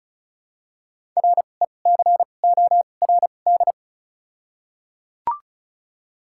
Morse code, heard as a keyed tone, about 34 wpm, spaced as for 25 wpm, 700 Hz. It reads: RECORD
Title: 2x Character Spacing - Top 500 Words - 35wpm